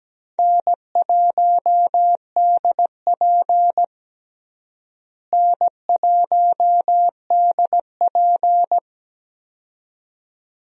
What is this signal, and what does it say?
Morse code, heard as a keyed tone, about 17 wpm, 700 Hz. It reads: N1DP N1DP
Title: USA Call Signs - 4 Characters - Repeated 2x - 17wpm